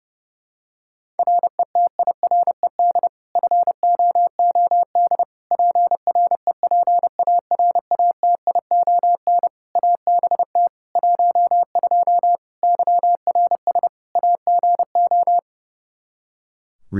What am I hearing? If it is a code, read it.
Morse code, heard as a keyed tone, about 29 wpm, 700 Hz. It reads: RETIRED FOOD PREPARATION ABT 12 YRS AGO
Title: K1USN SST QSO Practice 30wpm